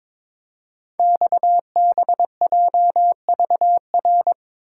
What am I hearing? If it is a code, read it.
Morse code, heard as a keyed tone, about 23 wpm, 690 Hz. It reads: XBJVR
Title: Code Groups - 5 Letters - 22wpm